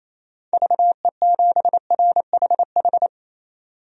Morse code, heard as a keyed tone, about 28 wpm, 700 Hz. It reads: VE7RHH
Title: DXCC-100 Call Signs - 6 Characters - 28wpm